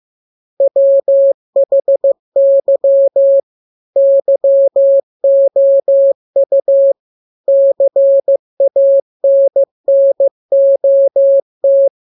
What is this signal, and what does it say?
Morse code, heard as a keyed tone, about 15 wpm, 550 Hz. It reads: WHY YOU CANNOT
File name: jokes / joke 20 15 0000